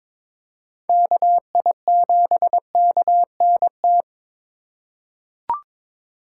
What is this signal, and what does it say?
Morse code, heard as a keyed tone, about 22 wpm, 700 Hz. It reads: KI7KNT
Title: USA Call Signs - 6 Characters - 8x Character Spacing - 22wpm